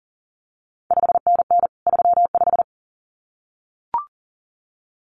Morse code, heard as a keyed tone, about 37 wpm, 700 Hz. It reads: 5NN 35
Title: CQ WW DX Contest Exchange Encore - 40wpm